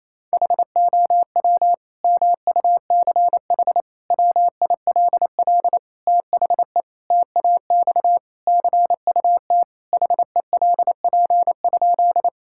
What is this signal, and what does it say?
Morse code, heard as a keyed tone, about 28 wpm, 700 Hz. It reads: HOW MUCH WILL THE TAX CUT HELP?
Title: Sentences from Top 1000 Words - Review Part 5 - 28wpm